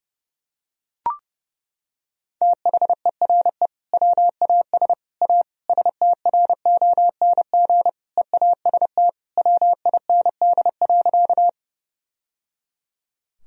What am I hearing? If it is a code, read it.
Morse code, heard as a keyed tone, about 30 wpm, 700 Hz. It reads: THERE WAS A STRONG EAST WIND.
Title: Sentences from Top 1000 Words - Review Part 3 - 30wpm